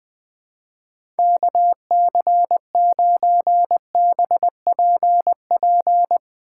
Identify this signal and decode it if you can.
Morse code, heard as a keyed tone, about 20 wpm, 700 Hz. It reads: KC9BPP
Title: USA Call Signs - 6 Characters - 2x Character Spacing - 20wpm